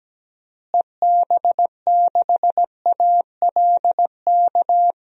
Morse code, heard as a keyed tone, about 17 wpm, 700 Hz. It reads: EB6ALK
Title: DXCC-100 Call Signs - 6 Characters - Repeated 3x - 17wpm